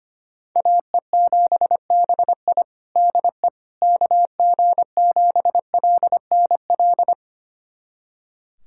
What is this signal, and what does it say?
Morse code, heard as a keyed tone, about 25 wpm, 700 Hz. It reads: AE7BI DE KG7LNL
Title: USA Callsign DE Callsign 25wpm